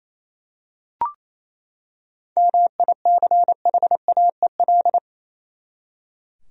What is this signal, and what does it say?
Morse code, heard as a keyed tone, about 28 wpm, 700 Hz. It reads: MICHAEL
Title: USA Common Names - 28wpm